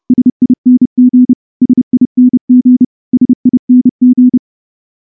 Morse code, heard as a keyed tone, about 28 wpm, 250 Hz. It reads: SING SING SING
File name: morse-training-30WPM